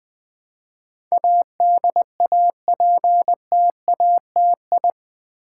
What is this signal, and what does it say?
Morse code, heard as a keyed tone, about 20 wpm, 700 Hz. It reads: ADAPTATI
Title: Build A Word - 4 Letters At A Time - 10-Letter Words - 20wpm